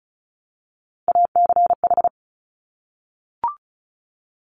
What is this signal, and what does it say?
Morse code, heard as a keyed tone, about 36 wpm, 700 Hz. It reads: ACH
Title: Top Three Letter Combinations in Words 35wpm